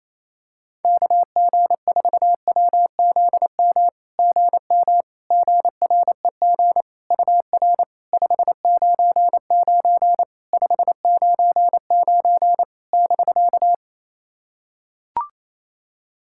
Morse code, heard as a keyed tone, about 28 wpm, 700 Hz. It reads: KG4WZM GM GREG UR 599 599 <BK>
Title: SOTA QSO Practice 28wpm